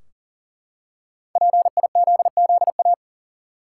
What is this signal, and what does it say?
Morse code, heard as a keyed tone, about 37 wpm, 700 Hz. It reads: PIZZA
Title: Five Letter Words - 40wpm